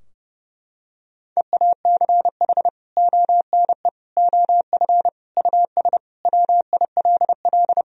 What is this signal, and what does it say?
Morse code, heard as a keyed tone, about 30 wpm, 700 Hz. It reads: EACH ONE OF US WILL
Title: Sentences from Top 1000 Words - Review Part 2 - 30wpm